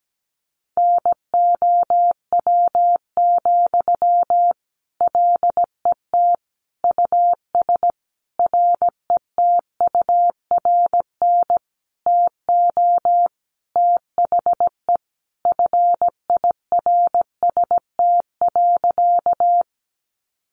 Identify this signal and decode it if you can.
Morse code, heard as a keyed tone, about 17 wpm, 700 Hz. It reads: NOW, LET US RETURN TO THE FIRST.
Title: Sentences from Top 800 Words 17wpm